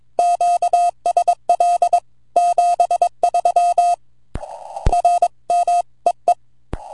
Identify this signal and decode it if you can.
Morse code, heard as a keyed tone, about 22 wpm, 700 Hz. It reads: QSL 73 R M EE
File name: vq9la / ok1fdy 10AUG08 0115Z 160mtr